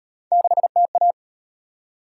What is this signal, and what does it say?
Morse code, heard as a keyed tone, about 31 wpm, 700 Hz. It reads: 6TA
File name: Speed-Racing - Three Letter-Number Combinations - 25wpm